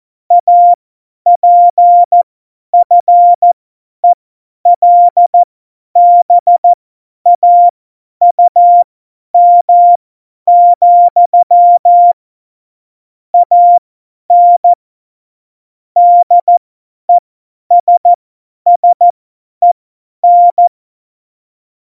Morse code, heard as a keyed tone, about 14 wpm, spaced as for 10 wpm, 700 Hz. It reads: APFELBAUM, AN DESSEN
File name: Morsekurs / Witze / Witz 17. November 2020, 50 F70